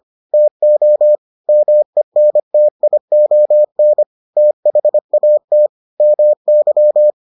Morse code, heard as a keyed tone, about 25 wpm, 600 Hz. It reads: TO MENTION THAT MY